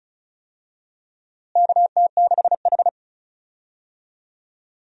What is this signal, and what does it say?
Morse code, heard as a keyed tone, about 36 wpm, 700 Hz. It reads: KT6H
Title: USA Call Signs - 4 Characters - Repeated 3x - 35wpm